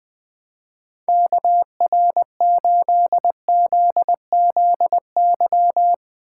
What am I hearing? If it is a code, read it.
Morse code, heard as a keyed tone, about 20 wpm, 700 Hz. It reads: KR8ZZY
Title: USA Call Signs - 6 Characters - 3x Character Spacing - 20wpm